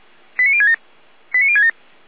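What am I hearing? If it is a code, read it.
Five-tone selective call (ZVEI), heard as quick runs of five tones, tones of about 70 ms. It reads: 89067 89067
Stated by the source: ZVEI five-tone call received by SDR, from feuerwehr-orlamuende.org (fire brigade Orlamünde).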